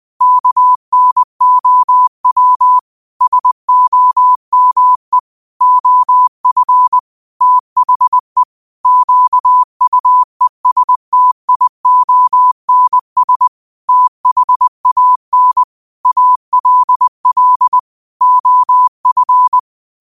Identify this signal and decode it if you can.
Morse code, heard as a keyed tone, about 20 wpm, 1000 Hz. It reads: KNOW SOME OF THE QUESTIONS THAN ALL OF
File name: qotdfiles / 20wpm